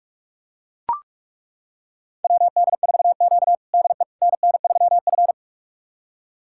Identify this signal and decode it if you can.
Morse code, heard as a keyed tone, about 42 wpm, 700 Hz. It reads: WD4Q DE NN3F
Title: USA Callsign DE Callsign - 4 Characters - 45wpm